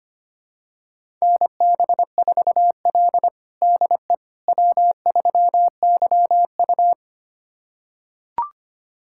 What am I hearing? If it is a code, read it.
Morse code, heard as a keyed tone, about 25 wpm, 700 Hz. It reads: NB4L DE W3YU
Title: USA Callsign DE Callsign - 4 Characters - 25wpm